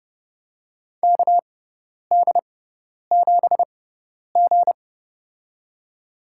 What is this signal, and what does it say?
Morse code, heard as a keyed tone, about 30 wpm, spaced as for 10 wpm, 700 Hz. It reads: KD7G
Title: USA Call Signs - 6x Character Spacing - 30wpm